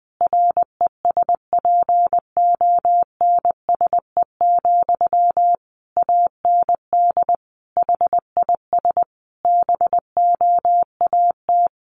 Morse code, heard as a keyed tone, about 19 wpm, 700 Hz. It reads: RESPONSE, AND HIS BOAT
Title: Sentences from Top 1400 Words 20wpm